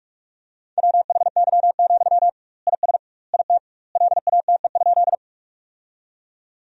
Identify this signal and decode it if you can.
Morse code, heard as a keyed tone, about 46 wpm, 700 Hz. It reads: WHY, IS IT LATE?
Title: 2x Word-Spacing - Sentences from Top 300 Words - 45wpm